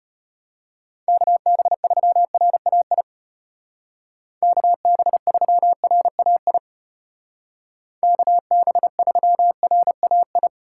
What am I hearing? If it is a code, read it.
Morse code, heard as a keyed tone, about 33 wpm, 700 Hz. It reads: KB3RAI KB3RAI KB3RAI